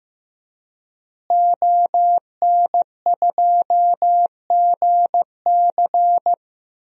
Morse code, heard as a keyed tone, about 15 wpm, 700 Hz. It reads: ON2GC
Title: DXCC-100 Call Signs - 5 Characters - Repeated 3x - 15wpm